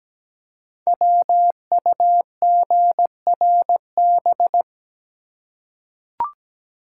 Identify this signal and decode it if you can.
Morse code, heard as a keyed tone, about 17 wpm, 700 Hz. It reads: WUGRB
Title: Code Groups - 5 Letters - 17wpm